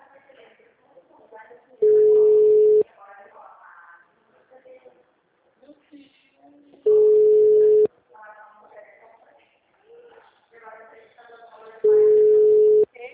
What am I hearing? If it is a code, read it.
Ringback tone (Europe, 425 Hz), with speech.